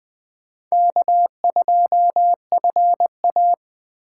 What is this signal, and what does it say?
Morse code, heard as a keyed tone, about 20 wpm, 700 Hz. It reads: K2FA
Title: USA Call Signs - 4 Characters - 4x Character Spacing - 20wpm